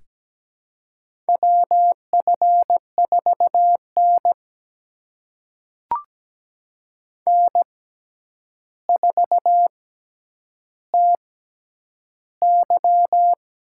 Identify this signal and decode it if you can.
Morse code, heard as a keyed tone, about 17 wpm, spaced as for 9 wpm, 700 Hz. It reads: WF4N N4TY
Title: USA Call Signs - 4 Characters - 6x Character Spacing - 17wpm